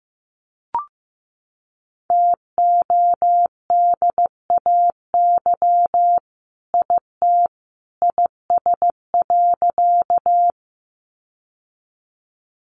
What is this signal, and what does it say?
Morse code, heard as a keyed tone, about 15 wpm, 700 Hz. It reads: TODAY IT IS.
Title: Sentences from Top 600 Words 15wpm